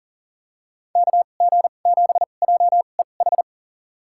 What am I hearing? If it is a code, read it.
Morse code, heard as a keyed tone, about 40 wpm, spaced as for 29 wpm, 700 Hz. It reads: KG7JEH
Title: USA Call Signs - 6 Characters - 2x Character Spacing - 40wpm